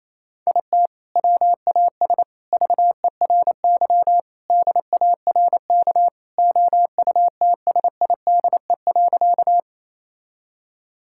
Morse code, heard as a keyed tone, about 28 wpm, 700 Hz. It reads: IT WAS VERY DARK OUTSIDE.
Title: Sentences from Top 1000 Words - Review Part 6 - 28wpm